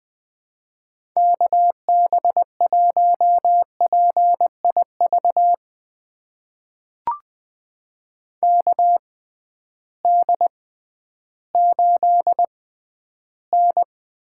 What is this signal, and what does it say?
Morse code, heard as a keyed tone, about 20 wpm, spaced as for 12 wpm, 700 Hz. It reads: KB1PIV KD8N
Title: USA Call Signs - 6 Characters - 6x Character Spacing - 20wpm